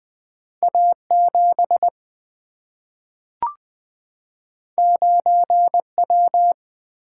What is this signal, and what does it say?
Morse code, heard as a keyed tone, about 20 wpm, 700 Hz. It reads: A 7 9 W
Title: Single-Letter Single-Number 20wpm